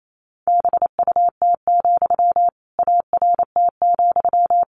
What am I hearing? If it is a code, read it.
Morse code, heard as a keyed tone, about 28 wpm, 700 Hz. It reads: BUT, ART,
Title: Sentences from Top 600 Words 28wpm